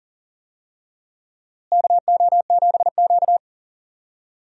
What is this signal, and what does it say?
Morse code, heard as a keyed tone, about 40 wpm, 700 Hz. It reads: KO7Q
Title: USA Call Signs - 4 Characters - Repeated 2x - 40wpm